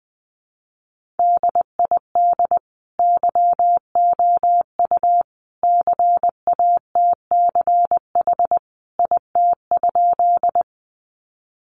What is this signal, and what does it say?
Morse code, heard as a keyed tone, about 20 wpm, 700 Hz. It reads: DID YOU CATCH IT?
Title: Sentences from Top 1500 Words 20wpm